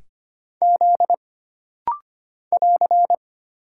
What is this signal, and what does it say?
Morse code, heard as a keyed tone, about 25 wpm, 700 Hz. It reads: Z <AR>
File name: Single Character (Letter:Number:Punctuation:Prosign) - Rapid-Fire - 25wpm